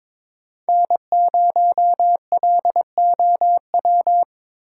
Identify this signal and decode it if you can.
Morse code, heard as a keyed tone, about 23 wpm, 700 Hz. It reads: N0LOW
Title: USA Call Signs - 5 Characters - 3x Character Spacing - 22wpm